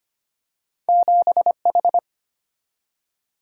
Morse code, heard as a keyed tone, about 25 wpm, 700 Hz. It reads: 7H
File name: Two Letter-Number Combinations - 25wpm